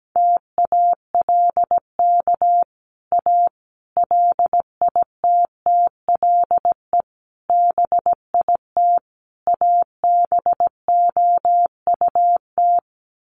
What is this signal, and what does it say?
Morse code, heard as a keyed tone, about 17 wpm, 700 Hz. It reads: TALK A LITTLE BIT ABOUT
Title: Sets of 5 Words Encore 17wpm